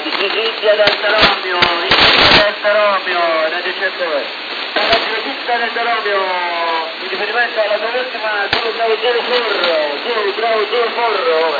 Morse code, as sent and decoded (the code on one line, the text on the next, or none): none